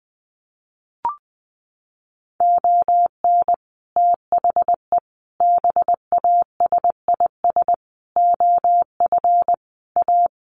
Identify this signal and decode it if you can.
Morse code, heard as a keyed tone, about 20 wpm, 700 Hz. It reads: ON THE BASIS OF A
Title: Sets of 5 Words Encore 20wpm